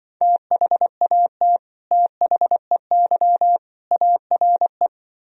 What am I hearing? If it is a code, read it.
Morse code, heard as a keyed tone, about 24 wpm, 700 Hz. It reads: THAT THEY ARE
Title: Speed-Racing - Sets of 3 Words 20wpm